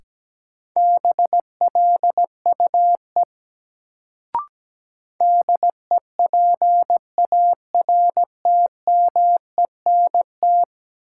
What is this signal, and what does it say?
Morse code, heard as a keyed tone, about 17 wpm, 700 Hz. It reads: BLUE DEPARTMENT
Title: Sentences from Top 1200 Words - New Word Drill Alt - 17wpm